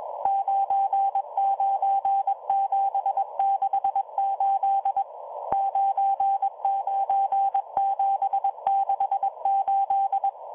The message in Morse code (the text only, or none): 99768 99768